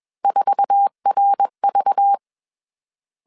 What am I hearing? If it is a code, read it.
Morse code, heard as a keyed tone, about 21 wpm, 800 Hz. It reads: 4RV